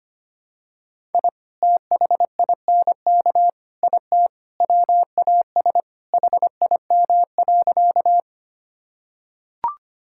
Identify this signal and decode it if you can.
Morse code, heard as a keyed tone, about 25 wpm, 700 Hz. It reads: I THINK IT WAS HIM.
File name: practice / 25wpm / Sentences with 4-Word Phrases - 25wpm